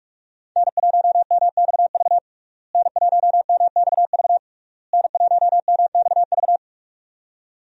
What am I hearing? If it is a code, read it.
Morse code, heard as a keyed tone, about 42 wpm, 700 Hz. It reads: N1MXV N1MXV N1MXV
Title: USA Call Signs - 5 Characters - Repeated 3x - 45wpm